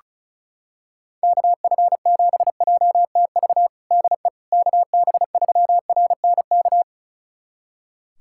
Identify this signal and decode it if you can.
Morse code, heard as a keyed tone, about 36 wpm, 700 Hz. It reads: KF7JTV DE KB3RNK
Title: USA Callsign DE Callsign 35wpm